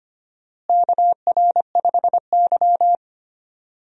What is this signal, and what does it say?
Morse code, heard as a keyed tone, about 25 wpm, 700 Hz. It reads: KR5Y
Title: USA Call Signs - 4 Characters - Repeated 2x - 25wpm